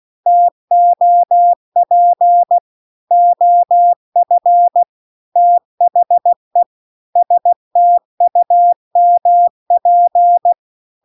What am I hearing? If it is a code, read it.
Morse code, heard as a keyed tone, about 16 wpm, 700 Hz. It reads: TOP OF THE STUMP